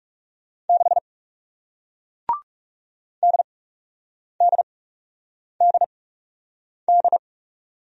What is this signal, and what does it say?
Morse code, heard as a keyed tone, about 38 wpm, spaced as for 8 wpm, 700 Hz. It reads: 6 DDDD